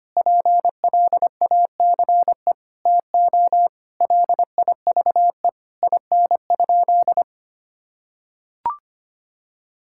Morse code, heard as a keyed tone, about 25 wpm, 700 Hz. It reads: PLACE TO LIVE IN?
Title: Sentences from Top 1000 Words - Review Part 5 - 25wpm